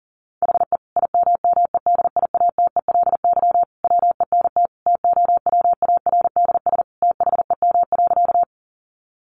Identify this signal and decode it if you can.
Morse code, heard as a keyed tone, about 40 wpm, 700 Hz. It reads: HE IMMEDIATELY WENT TOWARDS THEM.
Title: Sentences from Top 1100 Words 40wpm